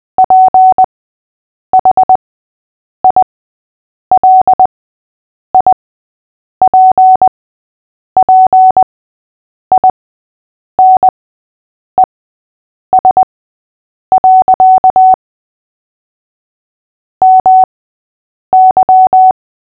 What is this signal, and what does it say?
Morse code, heard as a keyed tone, about 20 wpm, spaced as for 8 wpm, 750 Hz. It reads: PHILIPPINES. MY
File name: cw / fists / Short-Story-3-8wpm